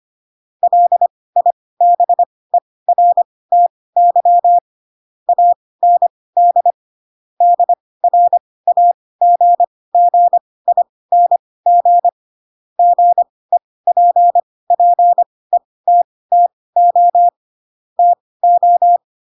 Morse code, heard as a keyed tone, about 25 wpm, spaced as for 18 wpm, 700 Hz. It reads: LIBERTY AND DRAGGING GEPPETTO TO
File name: Pinocchio ch3 12 25 ik1hge